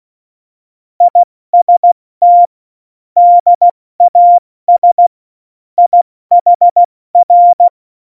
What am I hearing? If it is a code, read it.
Morse code, heard as a keyed tone, about 16 wpm, 700 Hz. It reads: IST DAS IHR
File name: Morsekurs / Witze / Witz 03.Dezember 2019, 70 F80